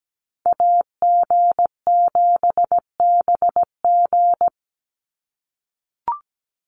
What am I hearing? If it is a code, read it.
Morse code, heard as a keyed tone, about 17 wpm, 700 Hz. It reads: AG7BG
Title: USA Call Signs 17wpm